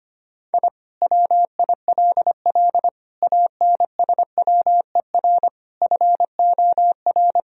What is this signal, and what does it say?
Morse code, heard as a keyed tone, about 25 wpm, 700 Hz. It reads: I WILL ANSWER FOR
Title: Sentences from Top 1000 Words - Review Part 4 - 25wpm